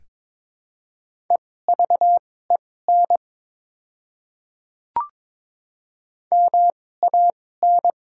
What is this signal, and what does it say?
Morse code, heard as a keyed tone, about 22 wpm, spaced as for 16 wpm, 700 Hz. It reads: EVEN MAN
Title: 2x Character Spacing - Top 200 Words - 22wpm